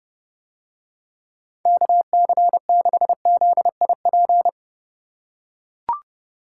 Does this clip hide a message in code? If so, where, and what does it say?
Yes: Morse code from 1.7 to 4.5 s, heard as a keyed tone, about 30 wpm, 700 Hz. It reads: KC6ZIP